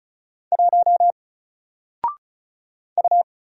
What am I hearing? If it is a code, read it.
Morse code, heard as a keyed tone, about 36 wpm, 700 Hz. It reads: 1 U
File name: Single-Letter-Number-35wpm